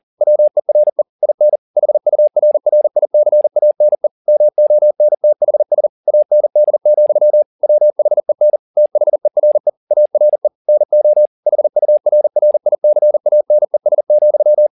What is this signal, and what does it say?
Morse code, heard as a keyed tone, about 40 wpm, 600 Hz. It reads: WERE IN HURRICANE MONTHS AND, WHEN THERE ARE NO HURRICANES,